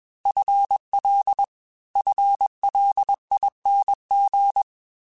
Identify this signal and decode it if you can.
Morse code, heard as a keyed tone, about 21 wpm, 800 Hz. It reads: FL FLING
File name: FL / FL mp3s / FL Prefixes 3x2 @20